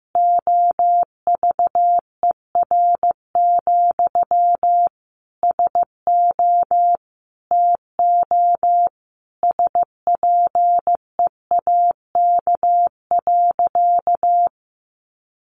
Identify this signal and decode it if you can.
Morse code, heard as a keyed tone, about 15 wpm, 700 Hz. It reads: OVER, SO TO SPEAK.
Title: Sentences from Top 1400 Words 15wpm